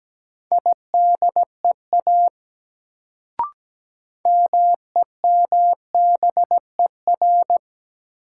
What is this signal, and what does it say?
Morse code, heard as a keyed tone, about 17 wpm, 700 Hz. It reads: IDEA MEMBER
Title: Sets of 3 Words - New Word Drill - 17wpm